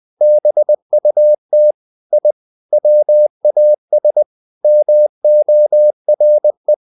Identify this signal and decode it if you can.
Morse code, heard as a keyed tone, about 20 wpm, 600 Hz. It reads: BUT I WAS MORE
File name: the old man and the sea part 2 20wpm